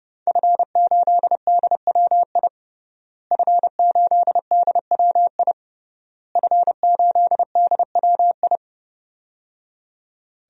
Morse code, heard as a keyed tone, about 30 wpm, 700 Hz. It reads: F8DWI F8DWI F8DWI
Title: DXCC-100 Call Signs - 5 Characters - Repeated 3x - 30wpm